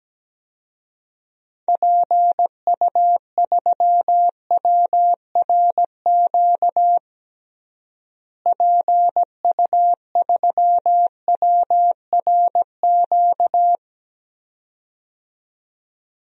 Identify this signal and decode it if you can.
Morse code, heard as a keyed tone, about 17 wpm, 700 Hz. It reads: PU3WRQ PU3WRQ
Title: DXCC-100 Call Signs - 6 Characters - Repeated 2x - 17wpm